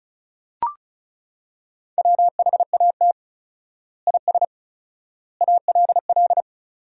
Morse code, heard as a keyed tone, about 36 wpm, 700 Hz. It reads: WHAT IS ALL
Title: 4x Word-Spacing - Sentences from Top 100 Words - 35wpm